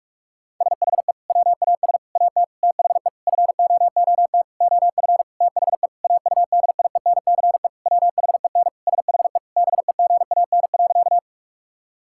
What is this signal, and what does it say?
Morse code, heard as a keyed tone, about 46 wpm, 700 Hz. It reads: SHE WAS AT THE FOOT OF THE AUDIENCE WHEN SHE BEGAN.
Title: Sentences from Top 1000 Words - Review Part 5 - 45wpm